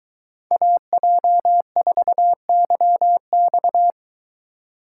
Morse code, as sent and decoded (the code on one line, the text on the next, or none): .- .--- ....- -.-- -..-
AJ4YX